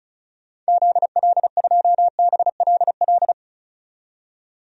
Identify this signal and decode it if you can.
Morse code, heard as a keyed tone, about 34 wpm, 700 Hz. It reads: ZL2BLL
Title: DXCC-100 Call Signs - 6 Characters - 35wpm